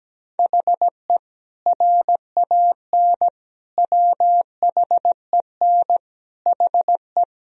Morse code, heard as a keyed tone, about 17 wpm, 700 Hz. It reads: HE RAN WHEN HE
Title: Sentences from Top 1000 Words - Review Part 3 - 17wpm